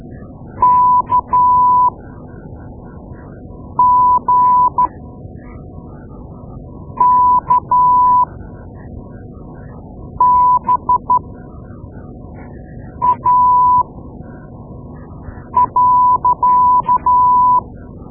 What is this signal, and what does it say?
Morse code, heard as a keyed tone, about 12 wpm, spaced as for 4 wpm, 990 Hz. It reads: KGKBA.
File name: Morse Lessons / Morse Lessons / Tape-2A